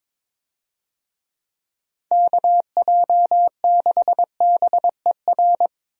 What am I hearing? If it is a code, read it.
Morse code, heard as a keyed tone, about 22 wpm, 700 Hz. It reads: KJ6BER
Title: USA Call Signs - 6 Characters - Repeated 3x - 22wpm